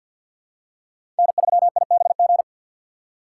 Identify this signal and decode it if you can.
Morse code, heard as a keyed tone, about 49 wpm, 700 Hz. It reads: N3IBG
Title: USA Call Signs - 5 Characters - 2x Character Spacing - 50wpm